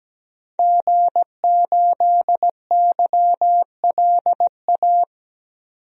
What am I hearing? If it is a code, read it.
Morse code, heard as a keyed tone, about 17 wpm, 700 Hz. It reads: G8YLA
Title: DXCC-100 Call Signs - 5 Characters - 17wpm